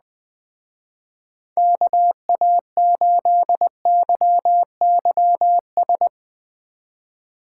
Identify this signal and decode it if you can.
Morse code, heard as a keyed tone, about 20 wpm, 700 Hz. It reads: KA8YYS
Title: Speed Racing - USA Call Signs 17wpm